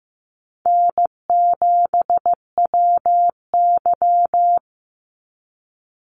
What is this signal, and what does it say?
Morse code, heard as a keyed tone, about 15 wpm, 700 Hz. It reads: N7WY
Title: Easy Call Signs Repeated 3x - 15wpm